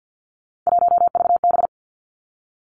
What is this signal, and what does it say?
Morse code, heard as a keyed tone, about 50 wpm, 700 Hz. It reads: 1VB
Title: Three Challenging Letter-Number Combinations 50wpm